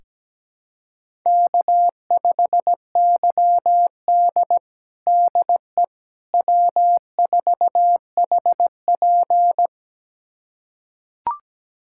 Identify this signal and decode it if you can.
Morse code, heard as a keyed tone, about 17 wpm, 700 Hz. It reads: K5YD DE W4HP
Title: USA Callsign DE Callsign - 4 Characters - 17wpm